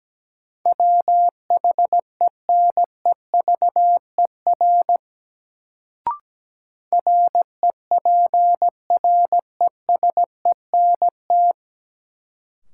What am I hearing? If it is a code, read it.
Morse code, heard as a keyed tone, about 17 wpm, 700 Hz. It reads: WHENEVER REPRESENT
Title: Sentences from Top 1200 Words - New Word Drill - 17wpm